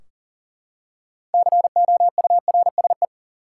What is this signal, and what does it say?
Morse code, heard as a keyed tone, about 40 wpm, 700 Hz. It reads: COURSE
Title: Six Letter Words - 40wpm